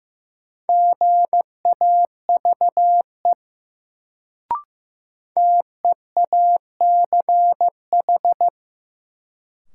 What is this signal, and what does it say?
Morse code, heard as a keyed tone, about 15 wpm, 700 Hz. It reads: GAVE TEACH
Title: Sentences from Top 500 Words - New Word Drill - 15wpm